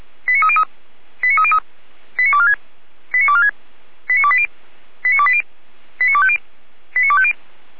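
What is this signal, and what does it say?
Five-tone selective call (ZVEI), heard as quick runs of five tones, tones of about 70 ms. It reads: 89303 89303 89246 89246 89280 89280 89250 89250
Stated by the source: ZVEI five-tone call 89223 received by SDR, from a Feuerwehr Orlamünde archive.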